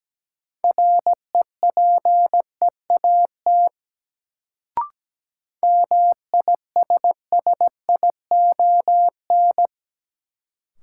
Morse code, heard as a keyed tone, about 17 wpm, 700 Hz. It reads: REPEAT MISSION
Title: Sentences from Top 1600 Words - New Word Drill Alt - 17wpm